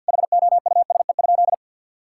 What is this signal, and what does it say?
Morse code, heard as a keyed tone, about 52 wpm, 700 Hz. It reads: HOUSE?
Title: Sentences from Top 1000 Words - Review Part 5 - 50wpm